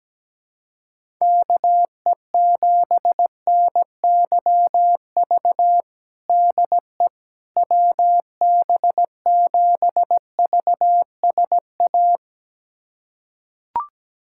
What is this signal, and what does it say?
Morse code, heard as a keyed tone, about 17 wpm, 700 Hz. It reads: KE7NYV DE WB7VSA
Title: USA Callsign DE Callsign 17wpm